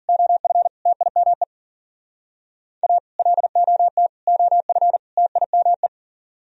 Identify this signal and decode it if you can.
Morse code, heard as a keyed tone, about 51 wpm, 700 Hz. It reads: OF TIME A LOT OF TIME